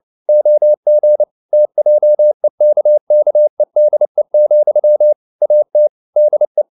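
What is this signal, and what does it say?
Morse code, heard as a keyed tone, about 29 wpm, 600 Hz. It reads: OG TJEKKEDE, AT DE
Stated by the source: Fast (QRQ) Morse at 29 wpm, from CW med Gnister.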